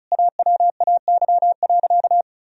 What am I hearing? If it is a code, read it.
Morse code, heard as a keyed tone, about 36 wpm, 700 Hz. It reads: AWAY.
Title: Sentences from Top 1000 Words - Review Part 4 - 35wpm